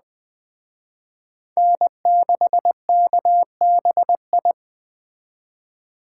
Morse code, heard as a keyed tone, about 22 wpm, 700 Hz. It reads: N6KBI